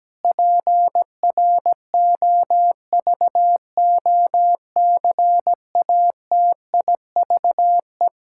Morse code, heard as a keyed tone, about 17 wpm, 700 Hz. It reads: PROVOCATIVE
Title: Pangram Challenge v2 - 17wpm